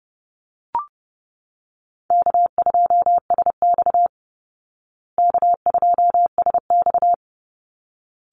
Morse code, heard as a keyed tone, about 28 wpm, 700 Hz. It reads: K2SX K2SX
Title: Call Signs Repeated 3x - 30wpm